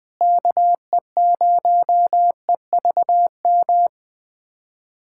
Morse code, heard as a keyed tone, about 20 wpm, 700 Hz. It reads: KE0EVM
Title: USA Callsign DE Callsign 20wpm